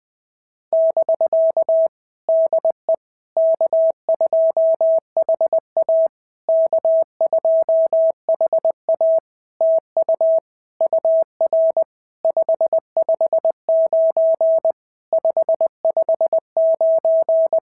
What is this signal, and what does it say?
Morse code, heard as a keyed tone, about 20 wpm, 650 Hz. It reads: <BK> DE K2HA K2HA TU UR 559 559
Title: POTA QSO Practice - 20wpm